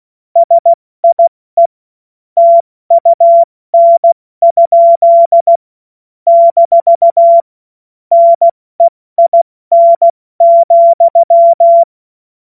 Morse code, heard as a keyed tone, about 16 wpm, spaced as for 14 wpm, 680 Hz. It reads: SIE TUN? - NEIN,
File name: Morsekurs / Witze / Witz 06. Dezember 2022 70 F80